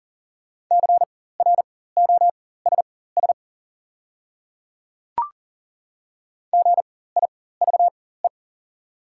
Morse code, heard as a keyed tone, about 40 wpm, spaced as for 19 wpm, 700 Hz. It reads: CROSS GIVE
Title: 4x Character Spacing - Top 300 Words - 40wpm